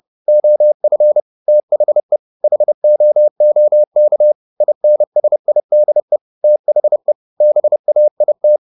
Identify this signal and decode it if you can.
Morse code, heard as a keyed tone, about 30 wpm, 600 Hz. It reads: OF THE HOOK INSIDE THE BAIT
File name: the old man and the sea / the old man and the sea 1 30wpm